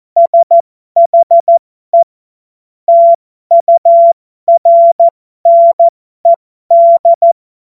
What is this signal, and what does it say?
Morse code, heard as a keyed tone, about 14 wpm, spaced as for 12 wpm, 680 Hz. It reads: SHE TURNED
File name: Morsekurs / Witze